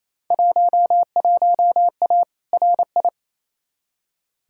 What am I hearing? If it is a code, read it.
Morse code, heard as a keyed tone, about 28 wpm, 700 Hz. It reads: 11A RI
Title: Field Day Exchanges Encore - 28wpm